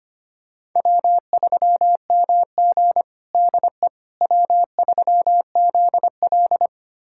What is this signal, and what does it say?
Morse code, heard as a keyed tone, about 25 wpm, 700 Hz. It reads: W3MG DE W3ZL
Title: USA Callsign DE Callsign - 4 Characters - 25wpm